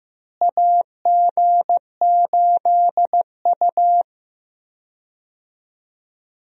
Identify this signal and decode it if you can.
Morse code, heard as a keyed tone, about 15 wpm, 700 Hz. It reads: AG8U
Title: USA Call Signs - 4 Characters - Repeated 3x - 15wpm